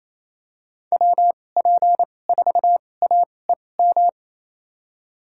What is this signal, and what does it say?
Morse code, heard as a keyed tone, about 28 wpm, spaced as for 20 wpm, 700 Hz. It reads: WP4AEM
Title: USA Call Signs - 2x Character Spacing - 28wpm